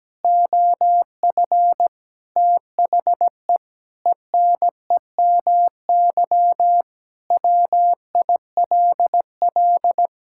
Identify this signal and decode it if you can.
Morse code, heard as a keyed tone, about 17 wpm, 700 Hz. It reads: OF THE ENEMY WILL
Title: Pangram Challenge v2 - 17wpm